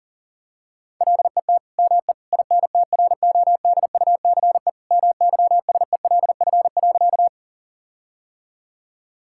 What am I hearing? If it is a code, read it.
Morse code, heard as a keyed tone, about 40 wpm, 700 Hz. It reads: LET ME INTRODUCE MYSELF.
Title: Sentences from Top 1600 Words 40wpm